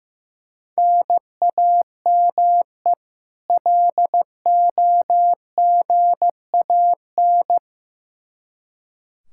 Morse code, heard as a keyed tone, about 15 wpm, 700 Hz. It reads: NAME LOGAN
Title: Practice QSO - Exchange 1 - Beginning - 15wpm